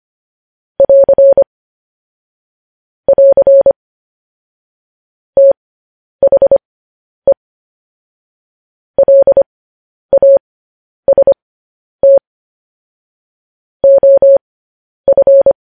Morse code, heard as a keyed tone, about 25 wpm, spaced as for 10 wpm, 550 Hz. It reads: <AR> <AR> THE LAST OF
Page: news headlines in morse code at 10 wpm